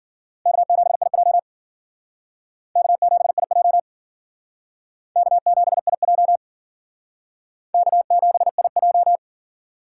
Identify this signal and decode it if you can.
Morse code, heard as a keyed tone, about 50 wpm, 700 Hz. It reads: K7IJ K7IJ K7IJ K7IJ